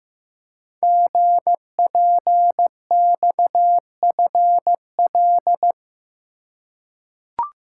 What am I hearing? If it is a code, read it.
Morse code, heard as a keyed tone, about 15 wpm, 700 Hz. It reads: GPXFL
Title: Code Groups - 5 Letters - 15wpm